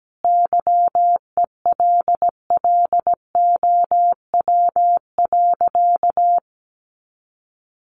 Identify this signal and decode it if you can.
Morse code, heard as a keyed tone, about 17 wpm, 700 Hz. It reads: YELLOW.
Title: Additional Easy Phrases 17wpm